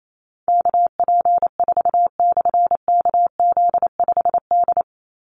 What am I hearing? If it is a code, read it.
Morse code, heard as a keyed tone, about 28 wpm, 700 Hz. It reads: KP4/KZ5D
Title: Call Signs Repeated 2x - 28wpm